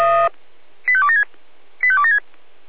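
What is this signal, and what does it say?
Five-tone selective call (ZVEI), heard as quick runs of five tones, tones of about 70 ms. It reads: 85276 85276
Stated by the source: ZVEI five-tone selective call 85248 received by SDR.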